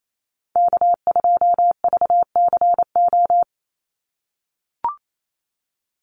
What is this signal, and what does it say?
Morse code, heard as a keyed tone, about 28 wpm, 700 Hz. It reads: K2VCO
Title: Call Signs - 28wpm